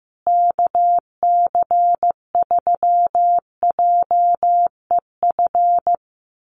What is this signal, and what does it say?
Morse code, heard as a keyed tone, about 15 wpm, 700 Hz. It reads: KC3JEF
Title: USA Call Signs Repeated 3x 15wpm